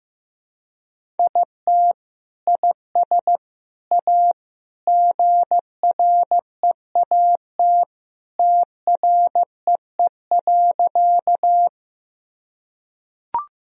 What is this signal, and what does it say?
Morse code, heard as a keyed tone, about 15 wpm, 700 Hz. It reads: IT IS A GREAT TREE.
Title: Sentences from Top 1000 Words - Review Part 1 - 15wpm